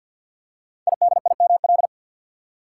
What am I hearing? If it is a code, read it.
Morse code, heard as a keyed tone, about 46 wpm, 700 Hz. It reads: IDIML
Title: Code Groups - 5 Letter:Numbers - 50wpm